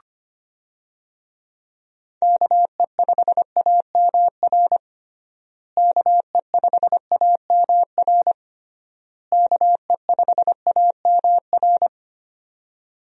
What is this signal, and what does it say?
Morse code, heard as a keyed tone, about 25 wpm, 700 Hz. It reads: KE5AMR KE5AMR KE5AMR
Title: USA Call Signs - 6 Characters - Repeated 3x - 25wpm